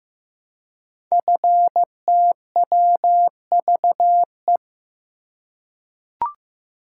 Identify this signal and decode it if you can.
Morse code, heard as a keyed tone, about 15 wpm, 700 Hz. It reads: FTWVE
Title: Code Groups - 5 Letters - 15wpm